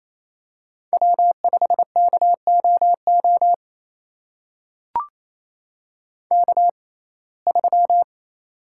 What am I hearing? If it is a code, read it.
Morse code, heard as a keyed tone, about 28 wpm, spaced as for 20 wpm, 700 Hz. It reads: W5KOO K3
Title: USA Call Signs - 5 Characters - 6x Character Spacing - 28wpm